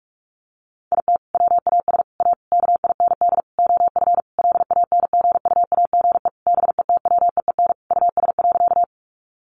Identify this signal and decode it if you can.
Morse code, heard as a keyed tone, about 44 wpm, 700 Hz. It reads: IT WAS A KIND OF LANGUAGE BETWEEN US.
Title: Sentences from Top 500 Words 45wpm